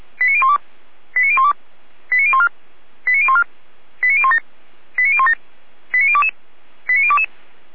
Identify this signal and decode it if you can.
Five-tone selective call (ZVEI), heard as quick runs of five tones, tones of about 70 ms. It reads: 89012 89012 89014 89014 89017 89017 89022 89022
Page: SDR/ZVEI